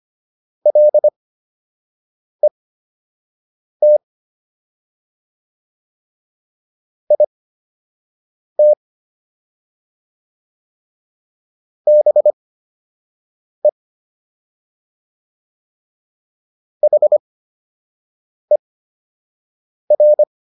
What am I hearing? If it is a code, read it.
Morse code, heard as a keyed tone, about 25 wpm, spaced as for 6 wpm, 600 Hz. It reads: LET IT BE HER